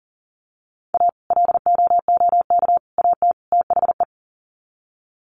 Morse code, heard as a keyed tone, about 40 wpm, 700 Hz. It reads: A LOOK AT THE
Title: Sets of 4 Words 40wpm